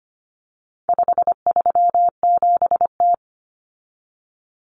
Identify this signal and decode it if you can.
Morse code, heard as a keyed tone, about 25 wpm, 700 Hz. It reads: 537T
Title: Common Cut Numbers - 4-Digit 25wpm